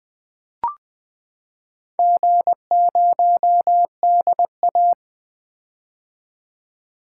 Morse code, heard as a keyed tone, about 20 wpm, 700 Hz. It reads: G0DA
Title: Four Letter-Number Combinations - 20wpm